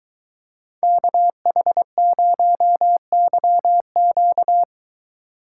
Morse code, heard as a keyed tone, about 22 wpm, 700 Hz. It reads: KH0YQ